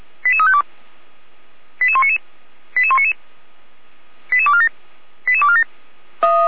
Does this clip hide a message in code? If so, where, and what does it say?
Yes: five-tone selective call (ZVEI) from 0.2 to 5.6 s, heard as quick runs of five tones, tones of about 70 ms. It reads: 88352 88199 88199 88247 88247